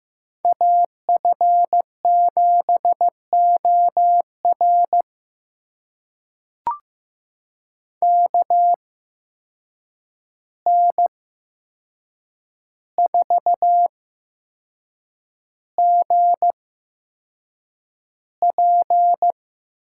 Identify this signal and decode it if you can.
Morse code, heard as a keyed tone, about 15 wpm, spaced as for 6 wpm, 700 Hz. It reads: AF7OR KN4GP
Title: USA Call Signs - 5 Characters - 8x Character Spacing - 15wpm